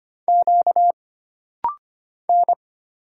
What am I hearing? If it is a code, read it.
Morse code, heard as a keyed tone, about 25 wpm, 700 Hz. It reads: Q N
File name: Single Character (Letter:Number:Punctuation:Prosign) - Rapid-Fire - 25wpm